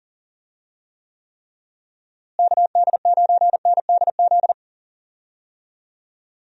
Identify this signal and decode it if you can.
Morse code, heard as a keyed tone, about 37 wpm, 700 Hz. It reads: KD9NDZ
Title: USA Call Signs - 6 Characters - Repeated 3x - 40wpm